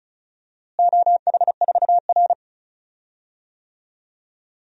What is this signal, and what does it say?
Morse code, heard as a keyed tone, about 34 wpm, 700 Hz. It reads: OH4R
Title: DXCC-100 Call Signs - 4 Characters - Repeated 3x - 35wpm